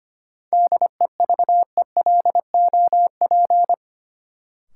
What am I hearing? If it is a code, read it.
Morse code, heard as a keyed tone, about 25 wpm, 700 Hz. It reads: DEVELOP
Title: Sentences from Top 1000 Words - Review Part 2 - New Word Drill - 25wpm